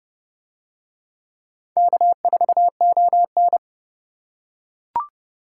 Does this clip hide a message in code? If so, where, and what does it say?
Yes: Morse code from 1.8 to 3.6 s, heard as a keyed tone, about 30 wpm, 700 Hz. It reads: K4ON